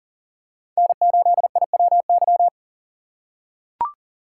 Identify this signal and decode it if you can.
Morse code, heard as a keyed tone, about 37 wpm, 700 Hz. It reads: N8IWY
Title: K1USN SST QSO Practice - 40wpm